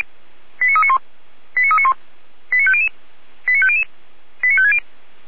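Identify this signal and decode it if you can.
Five-tone selective call (ZVEI), heard as quick runs of five tones, tones of about 70 ms. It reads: 89391 89391 89500 89500 89560
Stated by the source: ZVEI five-tone call 89338 received on SDR.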